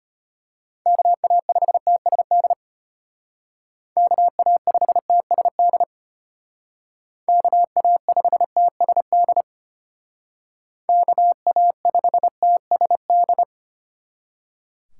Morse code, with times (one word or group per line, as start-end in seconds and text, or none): 0.9-2.5 KA5TSD
4.0-5.8 KA5TSD
7.3-9.4 KA5TSD
10.9-13.4 KA5TSD